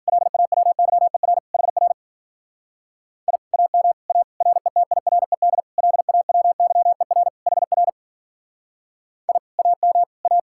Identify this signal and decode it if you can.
Morse code, heard as a keyed tone, about 52 wpm, 700 Hz. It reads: LAWYER HR I AM A RETIRED LAWYER HR I AM A